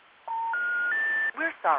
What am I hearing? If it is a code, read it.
Special information tone, with speech.